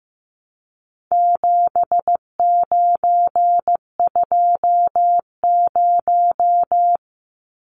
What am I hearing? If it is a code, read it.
Morse code, heard as a keyed tone, about 15 wpm, 700 Hz. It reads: 7920
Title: Numbers - 4-Digit 15wpm